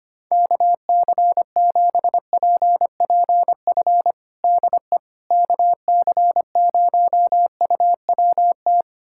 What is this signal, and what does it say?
Morse code, heard as a keyed tone, about 25 wpm, 700 Hz. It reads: KC7PPF DE KC0UWT
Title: USA Callsign DE Callsign - 6 Characters - 25wpm